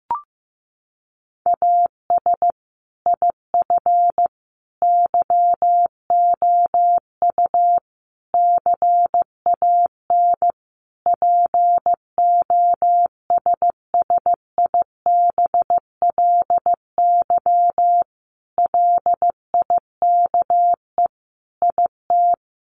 Morse code, heard as a keyed tone, about 14 wpm, 700 Hz. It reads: AS IF YOU CAN POSSIBLY LIKE IT
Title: Sentences from Top 1300 Words 15wpm